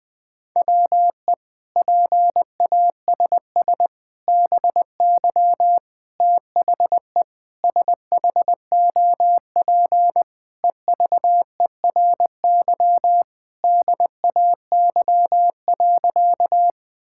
Morse code, heard as a keyed tone, about 21 wpm, 700 Hz. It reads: WE PASS BY THE SHOP EVERY DAY.
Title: Sentences from Top 1600 Words 20wpm